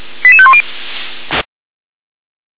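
Five-tone selective call (ZVEI), heard as quick runs of five tones, tones of about 70 ms. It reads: 88510
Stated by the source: ZVEI five-tone call received by SDR.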